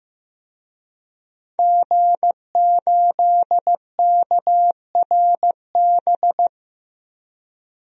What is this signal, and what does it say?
Morse code, heard as a keyed tone, about 15 wpm, 700 Hz. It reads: G8KRB
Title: DXCC-100 Call Signs - 5 Characters - Repeated 2x - 15wpm